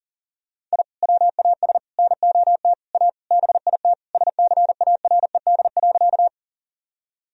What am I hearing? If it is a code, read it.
Morse code, heard as a keyed tone, about 40 wpm, 700 Hz. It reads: I WAS NOT A BIT SCARED.
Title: Sentences from Top 1000 Words - Review Part 4 - 40wpm